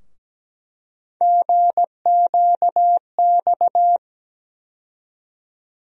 Morse code, heard as a keyed tone, about 17 wpm, 700 Hz. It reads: GQX
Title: USA Call Sign Postfixes - 3 Letters 17wpm